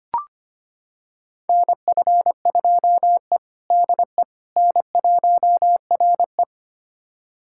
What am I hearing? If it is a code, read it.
Morse code, heard as a keyed tone, about 25 wpm, 700 Hz. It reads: NF2E DE N1RE
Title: USA Callsign DE Callsign - 4 Characters - 25wpm